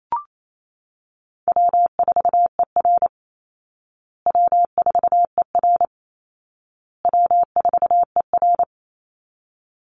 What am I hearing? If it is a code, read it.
Morse code, heard as a keyed tone, about 28 wpm, 700 Hz. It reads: W4ER W4ER W4ER
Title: Easy Call Signs Repeated 3x - 28wpm